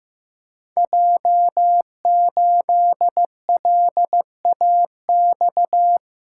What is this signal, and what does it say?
Morse code, heard as a keyed tone, about 15 wpm, 700 Hz. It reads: J8LAX
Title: Code Groups - 5 Letter:Numbers - 15wpm